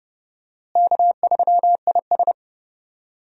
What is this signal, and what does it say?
Morse code, heard as a keyed tone, about 31 wpm, 700 Hz. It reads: K3IS
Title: USA Call Signs - 4 Characters - Repeated 2x - 30wpm